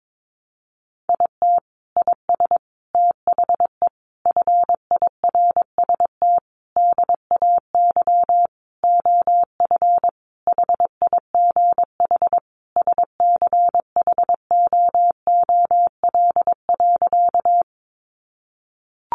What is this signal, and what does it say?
Morse code, heard as a keyed tone, about 22 wpm, 700 Hz. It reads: IT IS THE FIRST DAY OF HIGH SCHOOL.
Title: Additional Easy Phrases 22wpm